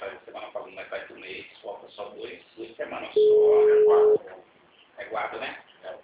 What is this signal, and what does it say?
Ringback tone (Europe, 425 Hz), with speech.
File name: out- -2021- - - .7605